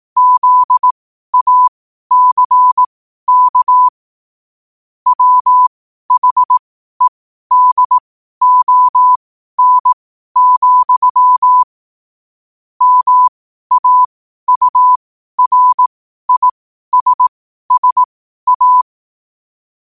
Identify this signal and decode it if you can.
Morse code, heard as a keyed tone, about 18 wpm, spaced as for 12 wpm, 1000 Hz. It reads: ZACK WHEDON, MAURISSA
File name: qotdfiles / 12wpm